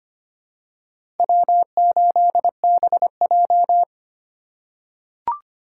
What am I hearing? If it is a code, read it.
Morse code, heard as a keyed tone, about 25 wpm, 700 Hz. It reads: W8BJ
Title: USA Call Signs - 4 Characters - 6x Character Spacing - 25wpm